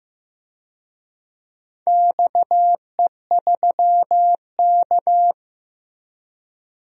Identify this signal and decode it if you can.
Morse code, heard as a keyed tone, about 15 wpm, 700 Hz. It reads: XE3K
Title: DXCC-100 Call Signs - 4 Characters - Repeated 2x - 15wpm